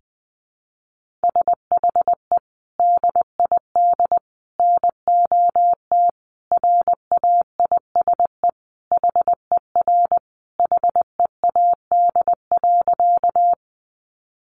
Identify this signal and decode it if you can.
Morse code, heard as a keyed tone, about 20 wpm, 700 Hz. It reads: SHE DID NOT RAISE HER HEAD.
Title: Sentences from Top 1400 Words 20wpm